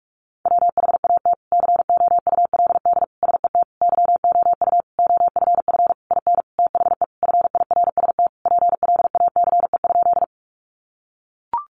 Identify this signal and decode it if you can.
Morse code, heard as a keyed tone, about 44 wpm, 700 Hz. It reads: WHAT COULD SET YOU OFF IN THE FIRST PLACE?
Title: Sentences from Top 200 Words 45wpm